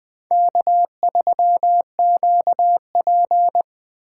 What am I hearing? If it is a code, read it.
Morse code, heard as a keyed tone, about 20 wpm, 700 Hz. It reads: K3QP
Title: Speed Racing - USA 1x2 Call Signs 17wpm